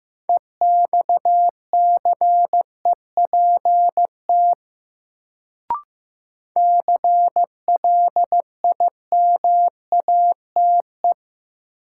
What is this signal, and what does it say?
Morse code, heard as a keyed tone, about 15 wpm, 700 Hz. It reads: EXCEPT CLIMATE
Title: Sentences from Top 1600 Words - New Word Drill - 15wpm